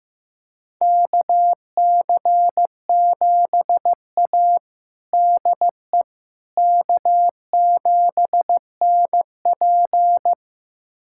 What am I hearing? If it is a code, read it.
Morse code, heard as a keyed tone, about 16 wpm, 700 Hz. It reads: KC7A DE K7NP
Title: USA Callsign DE Callsign - 4 Characters - 15wpm